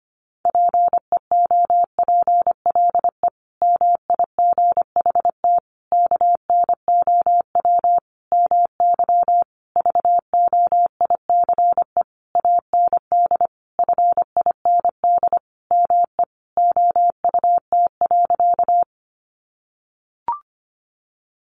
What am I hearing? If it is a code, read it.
Morse code, heard as a keyed tone, about 25 wpm, 700 Hz. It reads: PEOPLE MIGHT KNOW MY VOICE AND FIND ME OUT.
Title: Sentences from Top 500 Words 25wpm